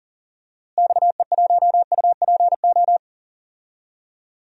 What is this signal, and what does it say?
Morse code, heard as a keyed tone, about 40 wpm, 700 Hz. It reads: XE1UPO
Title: DXCC-100 Call Signs - 6 Characters - Repeated 3x - 40wpm